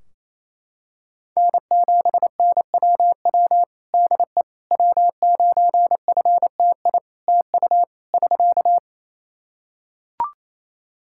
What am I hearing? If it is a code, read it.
Morse code, heard as a keyed tone, about 28 wpm, 700 Hz. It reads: N7NWW DE W9FTI TU <SK>
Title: Practice QSO - Exchange 4 - Ending - 28wpm